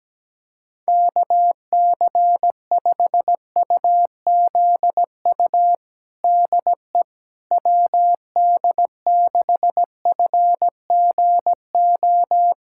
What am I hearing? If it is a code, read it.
Morse code, heard as a keyed tone, about 17 wpm, 700 Hz. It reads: KC5UZU DE WD6FGO
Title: USA Callsign DE Callsign 17wpm